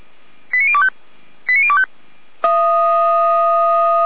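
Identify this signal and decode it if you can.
Five-tone selective call (ZVEI), heard as quick runs of five tones, tones of about 70 ms. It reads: 89025 89025
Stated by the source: ZVEI five-tone call 89338 received by SDR.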